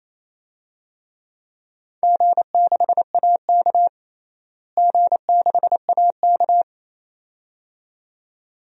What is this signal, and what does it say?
Morse code, heard as a keyed tone, about 27 wpm, 700 Hz. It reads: G6AK G6AK
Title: DXCC-100 Call Signs - 4 Characters - Repeated 2x - 28wpm